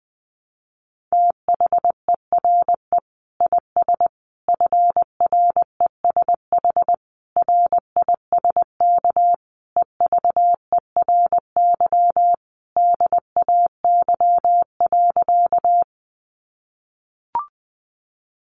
Morse code, heard as a keyed tone, about 19 wpm, 700 Hz. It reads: THERE IS FRESH RISK EVERY DAY.
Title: Sentences from Top 1200 Words 20wpm